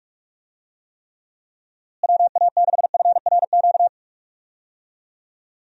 Morse code, heard as a keyed tone, about 41 wpm, 700 Hz. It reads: WA6FRQ
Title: USA Call Signs - 6 Characters - Repeated 2x - 45wpm